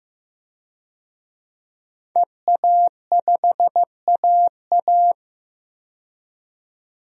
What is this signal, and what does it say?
Morse code, heard as a keyed tone, about 15 wpm, 700 Hz. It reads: EA5AA
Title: DXCC-100 Call Signs - 5 Characters - 15wpm